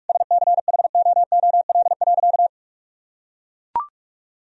Morse code, heard as a keyed tone, about 43 wpm, 700 Hz. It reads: SCHOOL.
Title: Sentences from Top 1000 Words - Review Part 1 - 45wpm